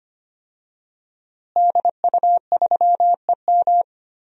Morse code, heard as a keyed tone, about 25 wpm, 700 Hz. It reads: DU3EM
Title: DXCC-100 Call Signs - 5 Characters - Repeated 3x - 25wpm